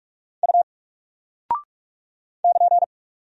Morse code, heard as a keyed tone, about 44 wpm, 700 Hz. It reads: U <KN>
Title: Single Character (Letter:Number:Punctuation:Prosign) Alt 45wpm